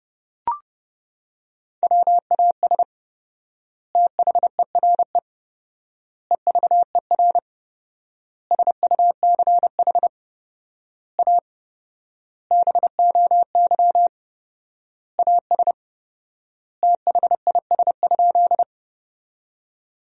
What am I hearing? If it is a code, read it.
Morse code, heard as a keyed tone, about 30 wpm, 700 Hz. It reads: WAS THERE EVER SUCH A BOY AS THIS?
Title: 4x Word-Spacing - Sentences from Top 400 Words - 30wpm